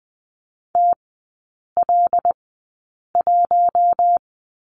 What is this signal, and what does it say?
Morse code, heard as a keyed tone, about 20 wpm, spaced as for 8 wpm, 700 Hz. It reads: TL1
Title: 087 Lesson 15e - Number 5 - Three Characters 20wpm